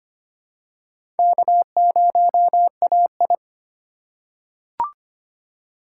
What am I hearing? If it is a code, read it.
Morse code, heard as a keyed tone, about 25 wpm, 700 Hz. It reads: K0AI
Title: USA Call Signs - 3x Character Spacing - 25wpm